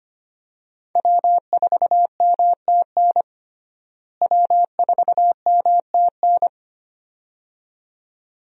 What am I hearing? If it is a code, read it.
Morse code, heard as a keyed tone, about 25 wpm, 700 Hz. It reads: W4MTN W4MTN
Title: USA Call Signs - 5 Characters - Repeated 2x - 25wpm